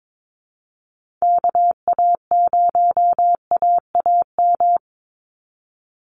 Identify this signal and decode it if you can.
Morse code, heard as a keyed tone, about 22 wpm, 700 Hz. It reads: KA0AAM
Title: Call Signs Repeated 2x - 22wpm